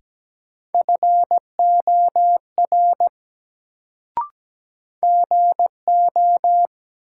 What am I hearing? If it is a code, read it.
Morse code, heard as a keyed tone, about 17 wpm, 700 Hz. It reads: FOR GO
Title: Sets of 2 Words - New Word Drill - 17wpm